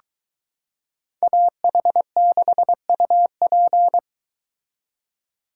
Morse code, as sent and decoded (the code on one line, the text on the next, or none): .- .... -.... ..- .--.
AH6UP